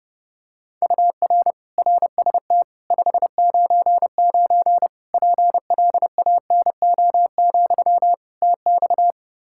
Morse code, heard as a keyed tone, about 30 wpm, 700 Hz. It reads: UR RST 599 PLANO, TX
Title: K1USN SST QSO Practice 30wpm